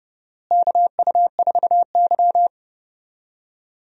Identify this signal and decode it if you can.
Morse code, heard as a keyed tone, about 31 wpm, 700 Hz. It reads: KU4Y
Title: USA Call Signs - 4 Characters - Repeated 2x - 30wpm